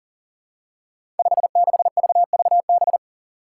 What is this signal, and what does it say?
Morse code, heard as a keyed tone, about 40 wpm, 700 Hz. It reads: 56VVB
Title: Five Challenging Letter-Number Combinations 40wpm